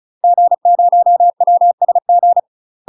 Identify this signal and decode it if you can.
Morse code, heard as a keyed tone, about 33 wpm, 700 Hz. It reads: G0WSG
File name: UK MASTER QSOs TRAINING / 35WPM / UK MASTER QSOs 0032